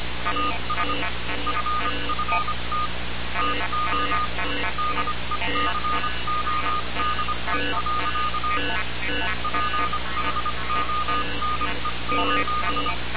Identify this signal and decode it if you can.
Morse code, heard as a keyed tone, about 25 wpm, 1220 Hz, with noise. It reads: AS FFT TO DECODE. PHONE:
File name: radiocap / REC z